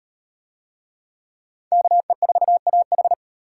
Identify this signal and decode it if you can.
Morse code, heard as a keyed tone, about 36 wpm, 700 Hz. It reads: KE4AH